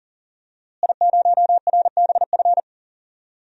Morse code, heard as a keyed tone, about 39 wpm, 700 Hz. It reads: I0RBF
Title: DXCC-100 Call Signs - 5 Characters - Repeated 2x - 40wpm